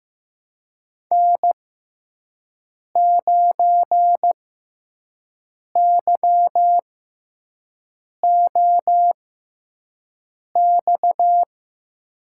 Morse code, heard as a keyed tone, about 15 wpm, spaced as for 5 wpm, 700 Hz. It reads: N9YOX
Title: USA Call Signs - 6x Character Spacing - 15wpm